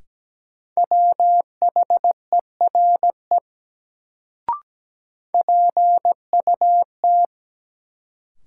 Morse code, heard as a keyed tone, about 17 wpm, 700 Hz. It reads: WHERE PUT
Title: Sentences from Top 200 Words - New Word Drill - 17wpm